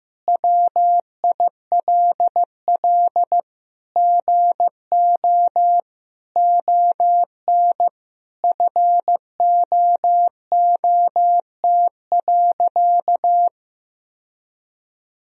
Morse code, heard as a keyed tone, about 15 wpm, 700 Hz. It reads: WILL GO ON FOOT.
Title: Sentences from Top 1000 Words - Review Part 3 - 15wpm